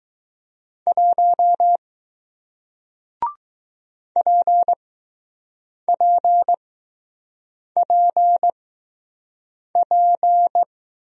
Morse code, heard as a keyed tone, about 20 wpm, spaced as for 6 wpm, 700 Hz. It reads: 1 PPPP